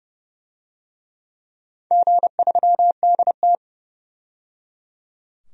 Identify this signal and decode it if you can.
Morse code, heard as a keyed tone, about 30 wpm, 700 Hz. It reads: G3DT
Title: DXCC-100 Call Signs - 4 Characters - 30wpm